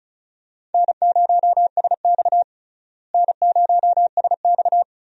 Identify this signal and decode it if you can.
Morse code, heard as a keyed tone, about 36 wpm, 700 Hz. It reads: N0SX N0SX
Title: USA Call Signs - 4 Characters - Repeated 3x - 35wpm